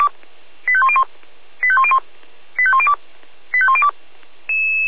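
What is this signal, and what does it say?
Five-tone selective call (ZVEI), heard as quick runs of five tones, tones of about 70 ms. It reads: 85101 85101 85102 85102
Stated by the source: Five-tone ZVEI selective call received by SDR.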